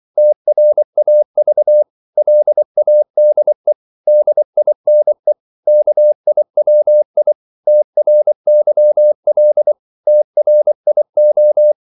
Morse code, heard as a keyed tone, about 24 wpm, 600 Hz. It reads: TRAV LADE DINE KIWI TRYL TRIO
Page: Korte danske ord 24wpm